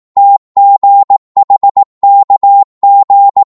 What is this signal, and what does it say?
Morse code, heard as a keyed tone, about 18 wpm, 800 Hz. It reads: TGHKG